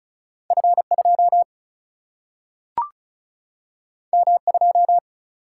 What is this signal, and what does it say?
Morse code, heard as a keyed tone, about 36 wpm, 700 Hz. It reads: F2 M2
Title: Single-Letter Single-Number 35wpm